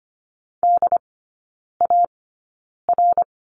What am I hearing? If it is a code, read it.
Morse code, heard as a keyed tone, about 25 wpm, spaced as for 9 wpm, 690 Hz. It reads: DAR
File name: Lesson 15e - Number 5 - Three Characters 25wpm